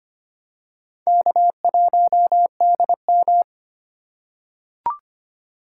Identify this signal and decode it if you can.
Morse code, heard as a keyed tone, about 25 wpm, 700 Hz. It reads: K1DM
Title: K1USN SST QSO Practice - 25wpm